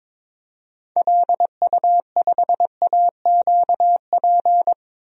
Morse code, heard as a keyed tone, about 21 wpm, 700 Hz. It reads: LU5AQP
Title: DXCC-100 Call Signs - 6 Characters - 22wpm